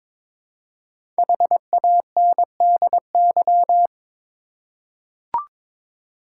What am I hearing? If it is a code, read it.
Morse code, heard as a keyed tone, about 22 wpm, 700 Hz. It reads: HANDY
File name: 5 Letter Words - First 3 Letters Given - 22wpm